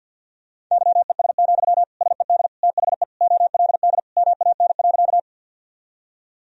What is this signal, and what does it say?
Morse code, heard as a keyed tone, about 52 wpm, 700 Hz. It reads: YES, SED THE OLD MAN.
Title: Sentences from Top 500 Words and Abbreviations 50wpm